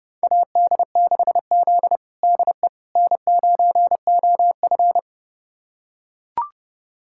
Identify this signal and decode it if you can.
Morse code, heard as a keyed tone, about 32 wpm, 700 Hz. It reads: AD6Z DE N9OF
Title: USA Callsign DE Callsign - 4 Characters - 30wpm